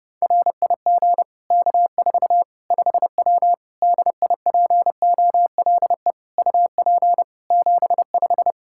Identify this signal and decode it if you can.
Morse code, heard as a keyed tone, about 30 wpm, 700 Hz. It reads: RIG K4 5W DIPOLE UP 75
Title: Practice QSO - Exchange 2 - Info - 30wpm